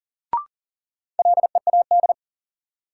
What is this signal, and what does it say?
Morse code, heard as a keyed tone, about 40 wpm, 700 Hz. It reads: LEAD
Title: Sentences from Top 500 Words - New Word Drill - 40wpm